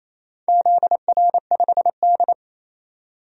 Morse code, heard as a keyed tone, about 28 wpm, 700 Hz. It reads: ZR5D
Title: DXCC-100 Call Signs - 4 Characters - 28wpm